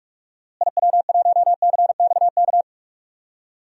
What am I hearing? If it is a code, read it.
Morse code, heard as a keyed tone, about 42 wpm, 700 Hz. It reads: IW1CXK
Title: DXCC-100 Call Signs - 6 Characters - Repeated 2x - 45wpm